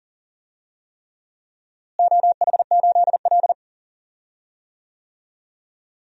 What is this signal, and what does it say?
Morse code, heard as a keyed tone, about 40 wpm, 700 Hz. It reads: OH8L
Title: DXCC-100 Call Signs - 4 Characters - Repeated 3x - 40wpm